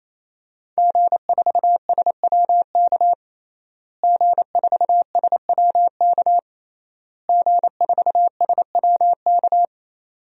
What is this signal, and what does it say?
Morse code, heard as a keyed tone, about 28 wpm, 700 Hz. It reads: G4SWK G4SWK G4SWK
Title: DXCC-100 Call Signs - 5 Characters - Repeated 3x - 28wpm